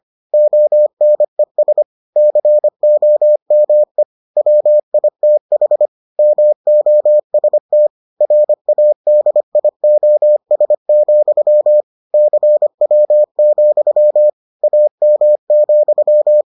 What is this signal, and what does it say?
Morse code, heard as a keyed tone, about 25 wpm, 600 Hz. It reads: ONES COME WITH MOST RADIOS, CW, AM,